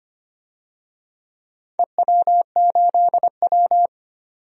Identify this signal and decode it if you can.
Morse code, heard as a keyed tone, about 25 wpm, 700 Hz. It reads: EW8W
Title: DXCC-100 Call Signs - 4 Characters - 25wpm